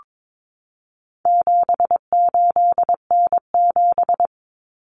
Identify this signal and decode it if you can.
Morse code, heard as a keyed tone, about 22 wpm, 700 Hz. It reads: 78N7
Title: Common Cut Numbers - 4-Digit 22wpm